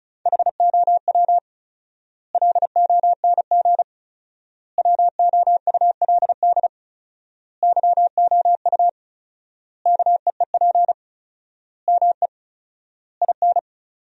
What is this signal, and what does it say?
Morse code, heard as a keyed tone, about 34 wpm, 700 Hz. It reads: HOW LONG WOULD YOU KEEP ME IN
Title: 4x Word-Spacing - Sentences from Top 500 Words - 35wpm